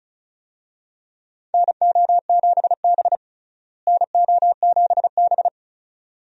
Morse code, heard as a keyed tone, about 34 wpm, 700 Hz. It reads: NO7B NO7B
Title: USA Call Signs - 4 Characters - Repeated 2x - 35wpm